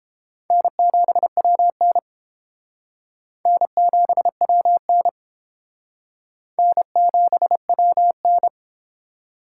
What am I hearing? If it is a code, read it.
Morse code, heard as a keyed tone, about 29 wpm, 700 Hz. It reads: N7WN N7WN N7WN